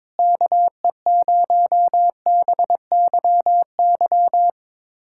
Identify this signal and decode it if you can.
Morse code, heard as a keyed tone, about 22 wpm, 700 Hz. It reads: KE0BYY
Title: USA Callsign DE Callsign - 6 Characters - 22wpm